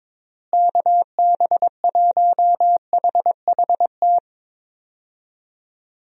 Morse code, heard as a keyed tone, about 22 wpm, 700 Hz. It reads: KB1HHT
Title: USA Call Signs - 6 Characters - Repeated 3x - 22wpm